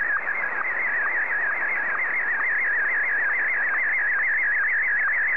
SSTV (slow-scan television), Martin 1.